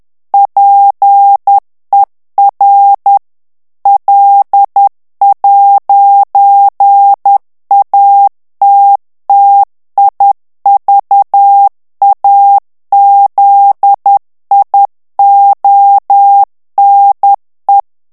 Morse code, heard as a keyed tone, about 10 wpm, 800 Hz. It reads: PER L'ATTIVAZIONE
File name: morse mp3 / ITALIEN texte clair 1 vitesse 10 mots minute